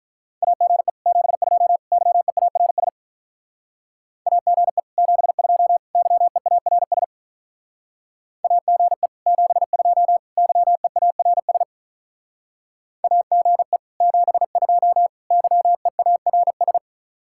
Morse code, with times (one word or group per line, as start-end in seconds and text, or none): 0.4-0.9 AGE
1.1-1.8 72
1.9-2.9 YEARS
4.3-4.8 AGE
5.0-5.8 72
6.0-7.0 YEARS
8.4-9.1 AGE
9.3-10.2 72
10.4-11.6 YEARS
13.0-13.8 AGE
14.0-15.1 72
15.3-16.8 YEARS